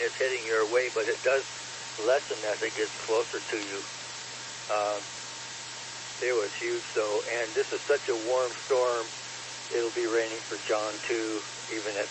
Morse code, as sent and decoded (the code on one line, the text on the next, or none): none